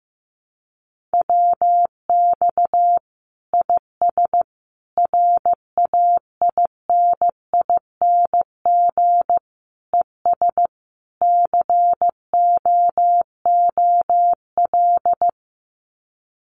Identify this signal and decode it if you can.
Morse code, heard as a keyed tone, about 15 wpm, 700 Hz. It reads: WX IS RAINING ES COOL
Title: Common QSO Phrases 15wpm